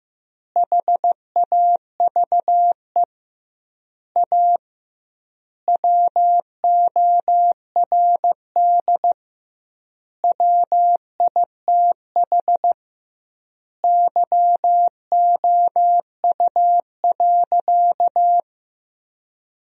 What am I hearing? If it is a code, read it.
Morse code, heard as a keyed tone, about 16 wpm, 700 Hz. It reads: HAVE A WORD WITH YOU.
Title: 2x Word-Spacing - Sentences from Top 300 Words - 15wpm